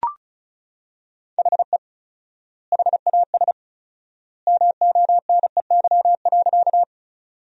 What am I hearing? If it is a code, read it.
Morse code, heard as a keyed tone, about 34 wpm, 700 Hz. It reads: HE HAS MONEY.
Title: 4x Word-Spacing - Sentences from Top 400 Words - 35wpm